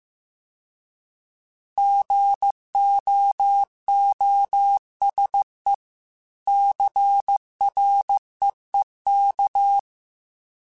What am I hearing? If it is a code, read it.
Morse code, heard as a keyed tone, about 15 wpm, 800 Hz. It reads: GOOSE CREEK